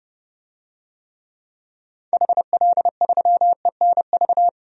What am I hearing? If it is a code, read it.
Morse code, heard as a keyed tone, about 31 wpm, 700 Hz. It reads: HL3ENV
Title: DXCC-100 Call Signs - 6 Characters - 30wpm